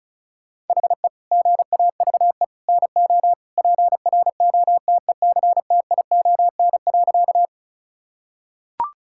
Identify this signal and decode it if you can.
Morse code, heard as a keyed tone, about 34 wpm, 700 Hz. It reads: HE GAVE NO PROTECTION.
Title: Sentences from Top 1600 Words 35wpm